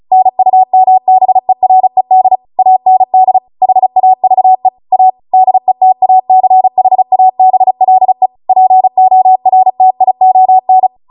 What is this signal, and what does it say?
Morse code, heard as a keyed tone, about 36 wpm, 750 Hz. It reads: NUMBERED AND HAVE A DETACHABLE PORTION
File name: Morse / 35WPM